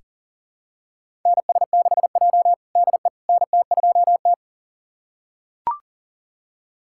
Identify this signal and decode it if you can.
Morse code, heard as a keyed tone, about 37 wpm, 700 Hz. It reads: NS6J DE NT2T
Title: USA Callsign DE Callsign - 4 Characters - 40wpm